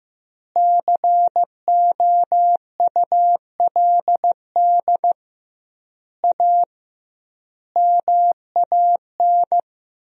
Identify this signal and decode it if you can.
Morse code, heard as a keyed tone, about 15 wpm, 700 Hz. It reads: COULD A MAN
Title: 2x Word-Spacing - Sentences from Top 200 Words - 15wpm